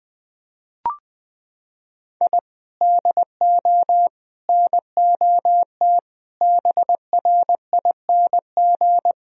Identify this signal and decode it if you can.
Morse code, heard as a keyed tone, about 21 wpm, 700 Hz. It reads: I DO NOT BRING
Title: Sentences from Top 1000 Words - Review Part 3 - 20wpm